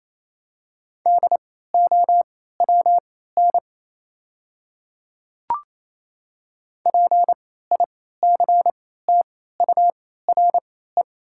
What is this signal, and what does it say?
Morse code, heard as a keyed tone, about 28 wpm, spaced as for 16 wpm, 700 Hz. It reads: DOWN PICTURE
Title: 3x Character Spacing - Top 200 Words - 28wpm